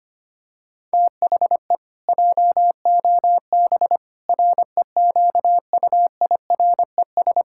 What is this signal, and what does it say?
Morse code, heard as a keyed tone, about 25 wpm, 700 Hz. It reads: THE JOB REQUIRES
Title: Pangram Challenge v2 - 25wpm